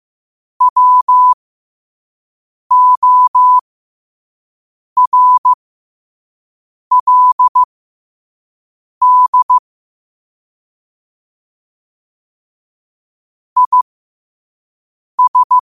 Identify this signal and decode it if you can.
Morse code, heard as a keyed tone, about 15 wpm, spaced as for 5 wpm, 1000 Hz. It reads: WORLD IS
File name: qotdfiles / 5wpm